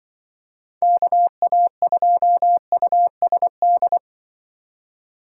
Morse code, heard as a keyed tone, about 23 wpm, 700 Hz. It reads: KA2USD
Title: Speed Racing - USA 2x3 Call Signs 20wpm